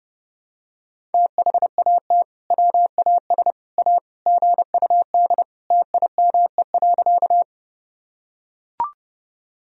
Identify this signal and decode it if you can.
Morse code, heard as a keyed tone, about 30 wpm, 700 Hz. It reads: THAT WAS A GUD TIME.
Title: Sentences from Top 200 Words and Abbreviations 30wpm